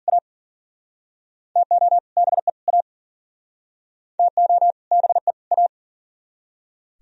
Morse code, heard as a keyed tone, about 45 wpm, 700 Hz. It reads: A TO BE A TO BE A
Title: Speed-Racing - Sets of 3 Words 40wpm